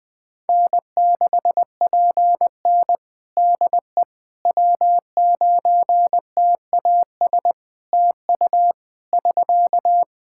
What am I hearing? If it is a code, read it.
Morse code, heard as a keyed tone, about 21 wpm, 700 Hz. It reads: N6PN DE W9TAS TU <SK>
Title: Practice QSO - Exchange 4 - Ending - 20wpm